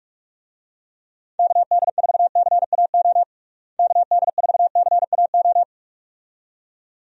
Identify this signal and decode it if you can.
Morse code, heard as a keyed tone, about 41 wpm, 700 Hz. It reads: KD4CAO KD4CAO
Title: USA Call Signs - 6 Characters - Repeated 2x - 45wpm